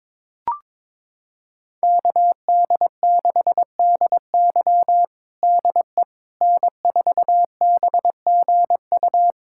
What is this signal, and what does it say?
Morse code, heard as a keyed tone, about 22 wpm, 700 Hz. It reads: KD6DY DE N4BGU
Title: USA Callsign DE Callsign - 5 Characters - 22wpm